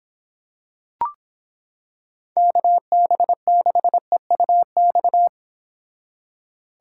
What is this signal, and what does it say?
Morse code, heard as a keyed tone, about 26 wpm, 700 Hz. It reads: KB6EUX